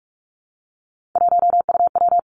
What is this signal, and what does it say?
Morse code, heard as a keyed tone, about 46 wpm, 700 Hz. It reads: 1UW
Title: Three Challenging Letter-Number Combinations 45wpm